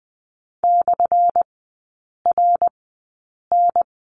Morse code, heard as a keyed tone, about 20 wpm, spaced as for 8 wpm, 700 Hz. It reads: /RN
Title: 191 Lesson 31e - Letter v - Three Characters 20wpm